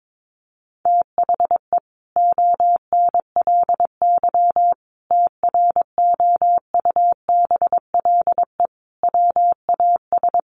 Morse code, heard as a keyed tone, about 22 wpm, 700 Hz. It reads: THE ONLY TROUBLE WAS
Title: Sentences from Top 1200 Words 22wpm